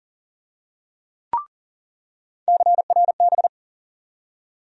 Morse code, heard as a keyed tone, about 40 wpm, 700 Hz. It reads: CRB
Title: USA Call Sign Postfixes - 3 Letters 40wpm